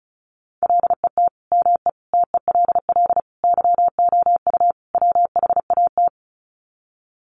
Morse code, heard as a keyed tone, about 34 wpm, 700 Hz. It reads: LET ME TELL YOU WHAT
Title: Sets of 5 Words Encore 35wpm